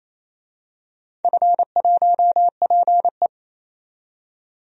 Morse code, heard as a keyed tone, about 28 wpm, 700 Hz. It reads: F1PE
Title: DXCC-100 Call Signs - 4 Characters - 28wpm